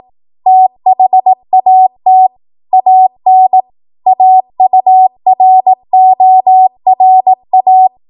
Morse code, heard as a keyed tone, about 18 wpm, 750 Hz. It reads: THAT AN AURORA
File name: Morse / 18WPM